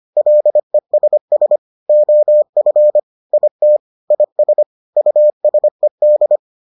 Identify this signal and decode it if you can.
Morse code, heard as a keyed tone, about 25 wpm, 600 Hz. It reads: LESS OF IT IS USED